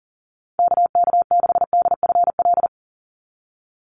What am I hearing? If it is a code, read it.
Morse code, heard as a keyed tone, about 40 wpm, 700 Hz. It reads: KK6DFL
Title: USA Call Signs 40wpm